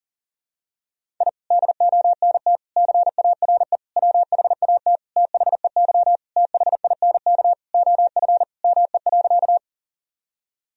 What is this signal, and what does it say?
Morse code, heard as a keyed tone, about 40 wpm, 700 Hz. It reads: I DONT CARE WHAT THEY THINK OF ME.
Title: Sentences from Top 1000 Words - Review Part 1 - 40wpm